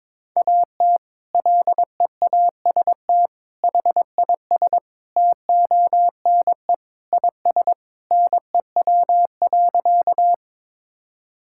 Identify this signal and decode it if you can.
Morse code, heard as a keyed tone, about 22 wpm, 700 Hz. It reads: AT LEAST HIS TONE IS NEW.
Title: Sentences from Top 1600 Words 22wpm